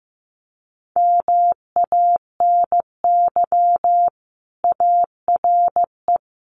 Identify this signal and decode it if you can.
Morse code, heard as a keyed tone, about 15 wpm, 700 Hz. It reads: MANY ARE
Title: Sentences from Top 1300 Words 15wpm